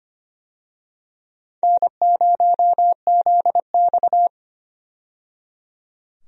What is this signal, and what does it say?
Morse code, heard as a keyed tone, about 25 wpm, 700 Hz. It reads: N0ZX
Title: USA Call Signs - 4 Characters - 25wpm